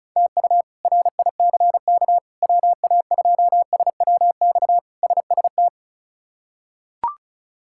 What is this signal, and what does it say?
Morse code, heard as a keyed tone, about 36 wpm, 700 Hz. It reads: TU RICK WA2SWX SST
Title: K1USN SST QSO Practice - 35wpm